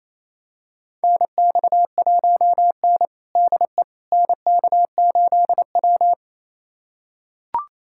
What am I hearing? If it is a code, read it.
Morse code, heard as a keyed tone, about 28 wpm, 690 Hz. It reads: NX1N DE NK8W
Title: USA Callsign DE Callsign - 4 Characters - 28wpm